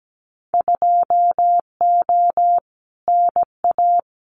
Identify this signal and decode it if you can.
Morse code, heard as a keyed tone, about 17 wpm, 700 Hz. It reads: 2O NA
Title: Winter Field Day Exchanges 17wpm